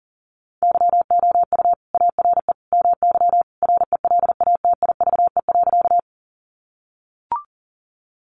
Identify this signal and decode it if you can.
Morse code, heard as a keyed tone, about 40 wpm, 700 Hz. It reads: YOU ARE MY RELATIVE.
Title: Sentences from Top 1400 Words 40wpm